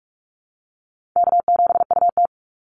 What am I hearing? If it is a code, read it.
Morse code, heard as a keyed tone, about 44 wpm, 700 Hz. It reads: K7UT
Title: Easy Call Signs - 45wpm